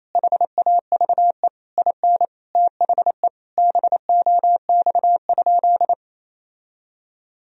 Morse code, heard as a keyed tone, about 28 wpm, 700 Hz. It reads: HAVE IN THE BOX?
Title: Sentences from Top 1000 Words - Review Part 2 - 28wpm